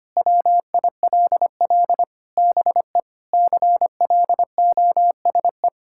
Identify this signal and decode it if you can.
Morse code, heard as a keyed tone, about 25 wpm, 700 Hz. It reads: WILL BE CLOSE
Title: Sentences from Top 1000 Words - Review Part 6 - 25wpm